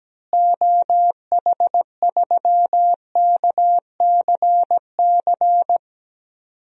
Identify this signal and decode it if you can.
Morse code, heard as a keyed tone, about 17 wpm, 700 Hz. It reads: OH3KCC
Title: DXCC-100 Call Signs - 6 Characters - Repeated 3x - 17wpm